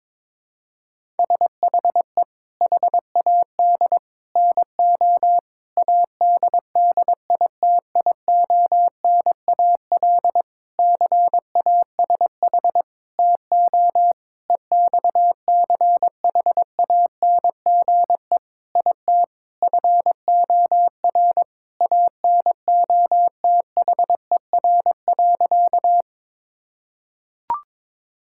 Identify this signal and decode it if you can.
Morse code, heard as a keyed tone, about 22 wpm, 700 Hz. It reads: SHE HAD NO ADDITIONAL CASH TO EXCHANGE IT FOR ANOTHER.
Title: Sentences from Top 1000 Words - Review Part 6 - 22wpm